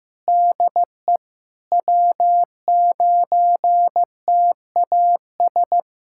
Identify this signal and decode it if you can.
Morse code, heard as a keyed tone, about 15 wpm, 700 Hz. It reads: DE W9TAS
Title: K1USN SST QSO Practice 15wpm